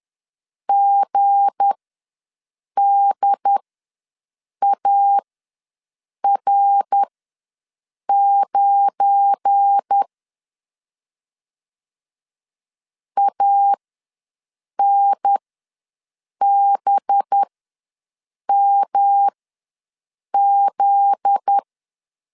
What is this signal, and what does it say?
Morse code, heard as a keyed tone, about 10 wpm, spaced as for 6 wpm, 800 Hz. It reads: GDAR9 ANBMZ
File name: LEZIONI / 7 GF9Z / 7 GF9Z ws 10wpm 3919